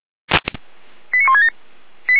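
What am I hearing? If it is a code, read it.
Five-tone selective call (ZVEI), heard as quick runs of five tones, tones of about 70 ms. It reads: 89167
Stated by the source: ZVEI five-tone call 89163 received via SDR.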